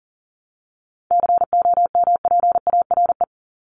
Morse code, heard as a keyed tone, about 40 wpm, 700 Hz. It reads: COMPARE
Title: ED Suffix - Top 100 Words with Base Words 40wpm